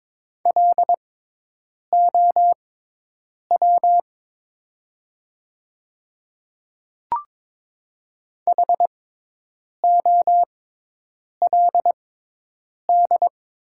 Morse code, heard as a keyed tone, about 22 wpm, spaced as for 8 wpm, 700 Hz. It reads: LOW HOLD
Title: 6x Character Spacing - Top 500 Words - 22wpm